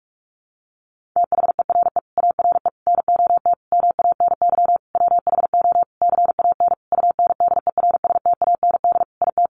Morse code, heard as a keyed tone, about 44 wpm, 700 Hz. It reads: THERE ARE NOT MANY WHO CAN UNDERSTAND IT
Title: Sentences from Top 1300 Words 45wpm